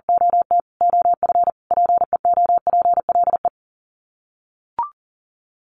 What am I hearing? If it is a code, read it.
Morse code, heard as a keyed tone, about 40 wpm, 700 Hz. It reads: OT OF PEOPLE
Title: Sets of 5 Words Encore2 40wpm